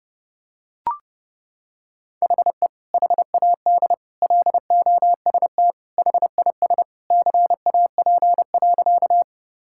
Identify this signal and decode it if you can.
Morse code, heard as a keyed tone, about 30 wpm, 700 Hz. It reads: HE HAD LOST HIS CAP.
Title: Sentences from Top 1600 Words 30wpm